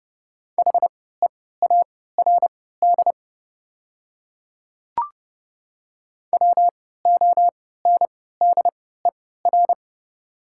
Morse code, heard as a keyed tone, about 30 wpm, spaced as for 17 wpm, 700 Hz. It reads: HEARD WONDER
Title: 3x Character Spacing - Top 500 Words - 30wpm